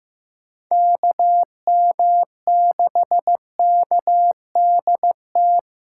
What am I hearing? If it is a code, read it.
Morse code, heard as a keyed tone, about 15 wpm, 700 Hz. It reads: KM6KDT
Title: USA Callsign DE Callsign - 6 Characters - 15wpm